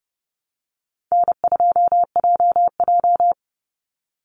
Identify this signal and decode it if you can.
Morse code, heard as a keyed tone, about 30 wpm, 700 Hz. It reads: N2JJ
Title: Easy Call Signs Repeated 3x - 30wpm